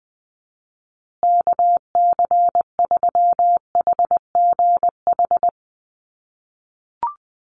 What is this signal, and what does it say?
Morse code, heard as a keyed tone, about 20 wpm, 700 Hz. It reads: KC3HGH
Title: USA Call Signs Repeated 2x 20wpm